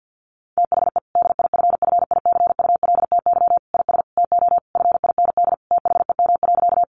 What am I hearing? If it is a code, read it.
Morse code, heard as a keyed tone, about 50 wpm, 700 Hz. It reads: THE DIFFICULTY IS TO FIND THEM.
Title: Sentences from Top 1500 Words 50wpm